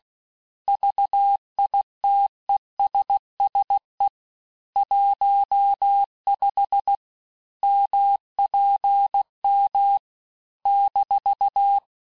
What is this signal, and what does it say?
Morse code, heard as a keyed tone, about 16 wpm, 800 Hz. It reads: VITESSE 15 MPM -
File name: cours-du-lundi files / cours-1592--teste-chiffre-vitesse-15mpm